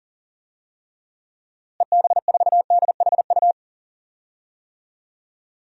Morse code, heard as a keyed tone, about 37 wpm, 700 Hz. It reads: EB4DHU
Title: DXCC-100 Call Signs - 6 Characters - Repeated 2x - 40wpm